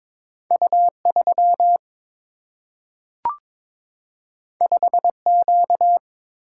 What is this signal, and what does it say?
Morse code, heard as a keyed tone, about 22 wpm, 700 Hz. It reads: U 3 5 Q
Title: Single-Letter Single-Number 22wpm